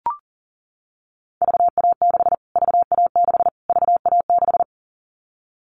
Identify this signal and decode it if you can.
Morse code, heard as a keyed tone, about 40 wpm, 700 Hz. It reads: VA6 VA6 VA6
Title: Callsign Prefixes for Top 25 DXCC - Repeated 3x - 40wpm